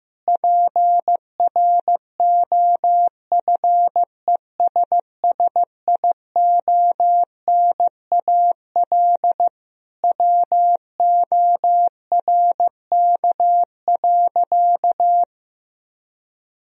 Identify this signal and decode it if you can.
Morse code, heard as a keyed tone, about 15 wpm, 700 Hz. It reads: PROFESSIONAL WORK.
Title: Sentences from Top 1000 Words - Review Part 4 - 15wpm